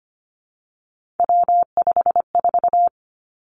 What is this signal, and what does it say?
Morse code, heard as a keyed tone, about 25 wpm, 700 Hz. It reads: W54
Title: Three Challenging Letter-Number Combinations 25wpm